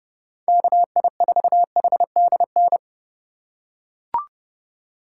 Morse code, heard as a keyed tone, about 31 wpm, 700 Hz. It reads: KI4HDN
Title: USA Call Signs - 6 Characters - 3x Character Spacing - 30wpm